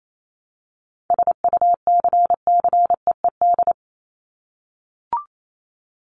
Morse code, heard as a keyed tone, about 28 wpm, 700 Hz. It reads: SUCCEED
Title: Top 1501-1750 Words 28wpm